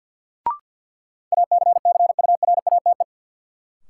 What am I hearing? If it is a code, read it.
Morse code, heard as a keyed tone, about 49 wpm, 700 Hz. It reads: ACCURATE
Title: Sentences from Top 1200 Words - New Word Drill - 50wpm